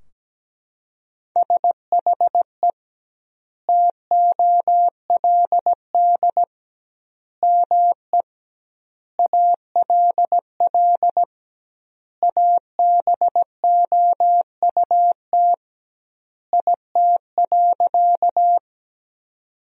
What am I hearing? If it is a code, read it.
Morse code, heard as a keyed tone, about 17 wpm, 700 Hz. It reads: SHE TOLD ME ALL ABOUT IT.
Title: 2x Word-Spacing - Sentences from Top 400 Words - 17wpm